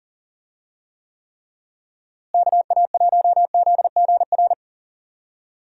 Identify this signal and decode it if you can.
Morse code, heard as a keyed tone, about 40 wpm, 700 Hz. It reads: KA1ZGR
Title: USA Call Signs - 6 Characters - Repeated 3x - 40wpm